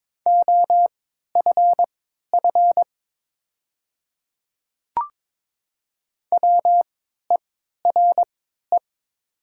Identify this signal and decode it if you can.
Morse code, heard as a keyed tone, about 22 wpm, spaced as for 12 wpm, 700 Hz. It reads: OFF WERE
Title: 3x Character Spacing - Top 300 Words - 22wpm